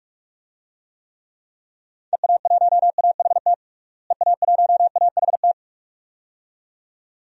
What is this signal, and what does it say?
Morse code, heard as a keyed tone, about 41 wpm, 700 Hz. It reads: EA1AHT EA1AHT
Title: DXCC-100 Call Signs - 6 Characters - Repeated 2x - 45wpm